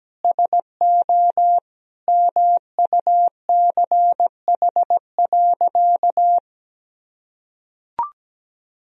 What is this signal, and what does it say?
Morse code, heard as a keyed tone, about 17 wpm, 700 Hz. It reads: SO MUCH.
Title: Sentences from Top 1000 Words - Review Part 3 - 17wpm